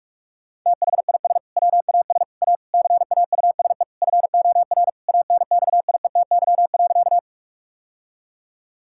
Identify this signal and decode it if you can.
Morse code, heard as a keyed tone, about 41 wpm, 700 Hz. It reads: THIS WAS A CAUSE FOR ANXIETY.
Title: Sentences from Top 1600 Words 45wpm